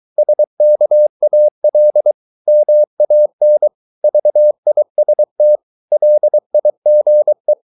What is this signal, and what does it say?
Morse code, heard as a keyed tone, about 23 wpm, 600 Hz. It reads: SKAL MAN VIST LIGE
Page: Ferie 23 wpm